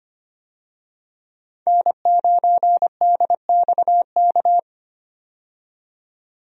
Morse code, heard as a keyed tone, about 25 wpm, 700 Hz. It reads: N9DXK
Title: USA Call Signs - 5 Characters - 25wpm